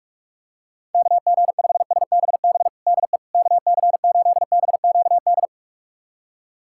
Morse code, heard as a keyed tone, about 46 wpm, 700 Hz. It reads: KG5SBB DE KC8BQD
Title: USA Callsign DE Callsign - 6 Characters - 45wpm